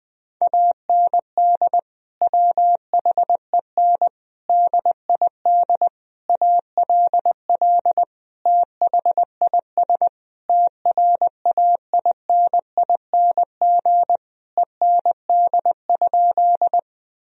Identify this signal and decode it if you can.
Morse code, heard as a keyed tone, about 21 wpm, 700 Hz. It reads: AND WHEN DID ALL THIS TRAINING END?
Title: Sentences from Top 1000 Words - Review Part 4 - 20wpm